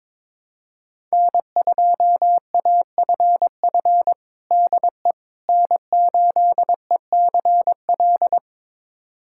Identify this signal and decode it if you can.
Morse code, heard as a keyed tone, about 22 wpm, 700 Hz. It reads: N2AFF DE N8ECL
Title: USA Callsign DE Callsign - 5 Characters - 22wpm